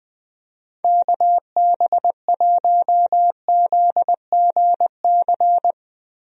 Morse code, heard as a keyed tone, about 20 wpm, 700 Hz. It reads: KB1ZGC
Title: USA Call Signs - 6 Characters - Repeated 2x - 20wpm